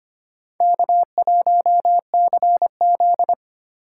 Morse code, heard as a keyed tone, about 25 wpm, 700 Hz. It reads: K1CZ
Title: USA Call Signs - 4 Characters - 8x Character Spacing - 25wpm